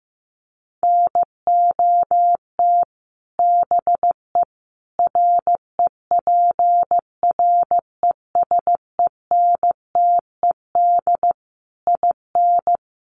Morse code, heard as a keyed tone, about 15 wpm, 700 Hz. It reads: NOT BE REPRESENTED IN
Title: Sets of 5 Words 15wpm